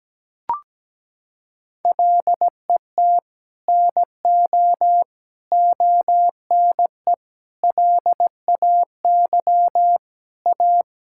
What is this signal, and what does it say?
Morse code, heard as a keyed tone, about 17 wpm, 700 Hz. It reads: LET NO ONE LAY A
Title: Sentences from Top 1600 Words 17wpm